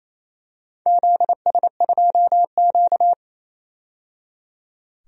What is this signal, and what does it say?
Morse code, heard as a keyed tone, about 29 wpm, 690 Hz. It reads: ZS2Q
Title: DXCC-100 Call Signs - 4 Characters - 28wpm